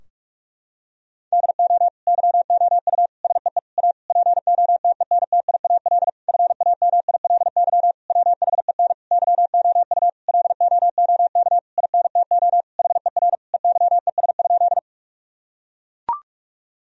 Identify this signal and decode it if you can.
Morse code, heard as a keyed tone, about 41 wpm, 700 Hz. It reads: DO YOU SEE A POTENTIAL FAMILY WHEN YOU LOOK INTO HER EYES?
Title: Sentences from Top 1000 Words - Review Part 4 - 45wpm